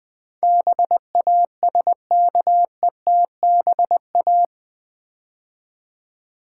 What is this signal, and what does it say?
Morse code, heard as a keyed tone, about 19 wpm, 700 Hz. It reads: BASKETBA
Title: Build A Word - 4 Letters At A Time - 10-Letter Words - 20wpm